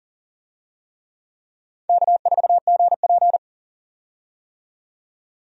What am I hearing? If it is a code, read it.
Morse code, heard as a keyed tone, about 40 wpm, 700 Hz. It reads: K4GP
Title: USA Call Signs - 4 Characters - 40wpm